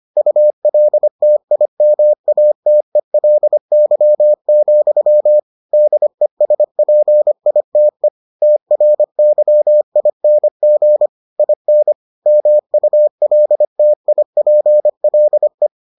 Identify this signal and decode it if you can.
Morse code, heard as a keyed tone, about 25 wpm, 600 Hz. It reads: ULTIMATELY, DESPITE TRYING IN MULTIPLE